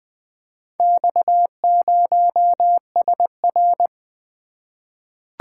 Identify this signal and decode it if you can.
Morse code, heard as a keyed tone, about 20 wpm, 700 Hz. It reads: X0SR
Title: Four Letter-Number Combinations - 20wpm